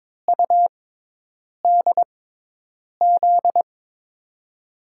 Morse code, heard as a keyed tone, about 22 wpm, spaced as for 8 wpm, 700 Hz. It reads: UDZ
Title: USA Call Signs - 6x Character Spacing - 22wpm